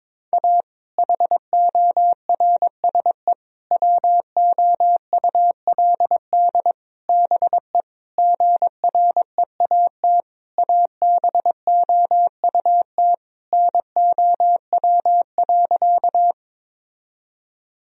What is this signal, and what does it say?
Morse code, heard as a keyed tone, about 22 wpm, 700 Hz. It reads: A HORSE WOULD BE GREAT ABOUT NOW.
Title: Sentences from Top 1000 Words - Review Part 1 - 22wpm